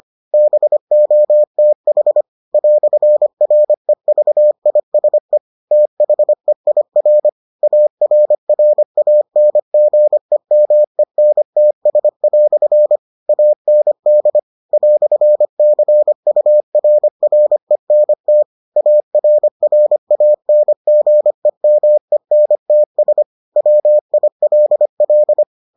Morse code, heard as a keyed tone, about 25 wpm, 600 Hz. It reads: BOTH "REVISE THEIR ARRANGEMENTS" AND "CURRENT ARRANGEMENTS WILL